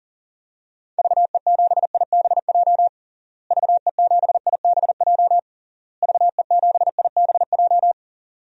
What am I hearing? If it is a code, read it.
Morse code, heard as a keyed tone, about 40 wpm, 700 Hz. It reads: VE7IBJ VE7IBJ VE7IBJ
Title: DXCC-100 Call Signs - 6 Characters - Repeated 3x - 40wpm